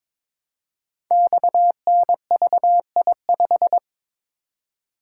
Morse code, heard as a keyed tone, about 22 wpm, 700 Hz. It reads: XNVI5
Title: Code Groups - 5 Letter:Numbers - 22wpm